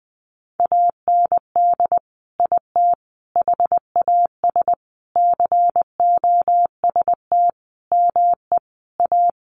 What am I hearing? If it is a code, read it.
Morse code, heard as a keyed tone, about 19 wpm, 700 Hz. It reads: AND IT HAS COST ME A
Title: Sentences from Top 1300 Words 20wpm